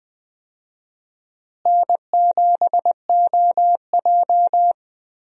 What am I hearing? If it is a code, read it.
Morse code, heard as a keyed tone, about 20 wpm, 700 Hz. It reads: N7OJ
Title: USA Call Signs - 4 Characters - Repeated 2x - 20wpm